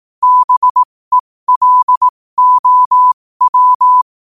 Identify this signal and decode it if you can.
Morse code, heard as a keyed tone, about 18 wpm, spaced as for 15 wpm, 1000 Hz. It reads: BELOW
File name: qotdfiles / 15wpm